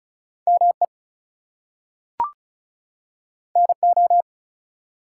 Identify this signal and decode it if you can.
Morse code, heard as a keyed tone, about 36 wpm, 700 Hz. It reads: ME NO
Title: Two Letter Words - 35wpm